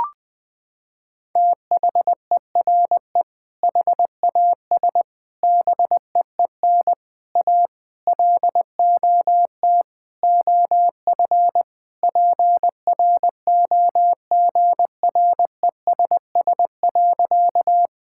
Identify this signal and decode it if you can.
Morse code, heard as a keyed tone, about 20 wpm, 700 Hz. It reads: THERE HAS BEEN A LOT OF PROGRESS.
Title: Sentences with 6-Word Phrases - 20wpm